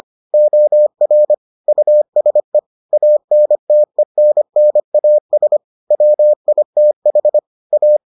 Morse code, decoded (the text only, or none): OR USE ANTENNAS WITH A